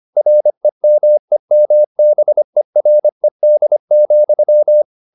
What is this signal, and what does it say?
Morse code, heard as a keyed tone, about 25 wpm, 600 Hz. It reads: REMEMBERED,
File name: the old man and the sea part 1 25wpm